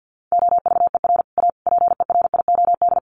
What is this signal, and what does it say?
Morse code, heard as a keyed tone, about 50 wpm, 700 Hz. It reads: OVER A PERIOD
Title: Sets of 5 Words Encore2 50wpm